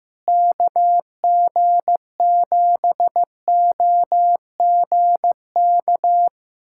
Morse code, heard as a keyed tone, about 15 wpm, 700 Hz. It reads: KG7OGK
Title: USA Callsign DE Callsign 15wpm